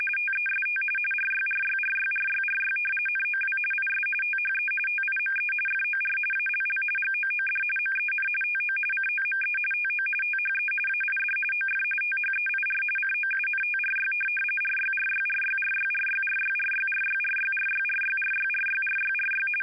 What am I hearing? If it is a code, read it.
RTTY, about 45.45 baud. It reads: TEST RYRYRYRYRY NOW IS THE TIME FOR ALL GOOD MEN TO COME TO THE AID OF THEIR COUNTRY RYRYRYRYRYRYRYRYRYRYRYRYRYRYRY
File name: Audio Files / All Good Men 60 wpm RTTY 850 Hz